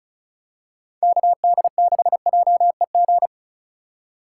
Morse code, heard as a keyed tone, about 36 wpm, 700 Hz. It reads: KD6JEG
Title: USA Call Signs - 2x Character Spacing - 35wpm